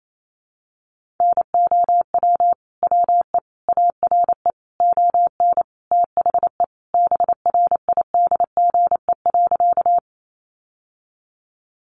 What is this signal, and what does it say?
Morse code, heard as a keyed tone, about 28 wpm, 700 Hz. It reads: NOW WE ARE ON THE BRIDGE.
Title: Sentences from Top 1200 Words 28wpm